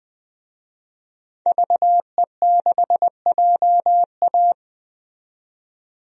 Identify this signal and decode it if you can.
Morse code, heard as a keyed tone, about 20 wpm, 700 Hz. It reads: VE6JA
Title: DXCC-100 Call Signs - 5 Characters - 20wpm